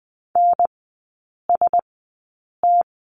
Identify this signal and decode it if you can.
Morse code, heard as a keyed tone, about 20 wpm, spaced as for 8 wpm, 700 Hz. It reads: NST
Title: 041 Lesson 8e - Number 1 - Three Characters 20wpm